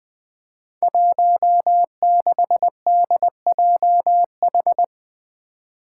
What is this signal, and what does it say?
Morse code, heard as a keyed tone, about 20 wpm, 700 Hz. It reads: 16DJH
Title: Five Challenging Letter-Number Combinations 20wpm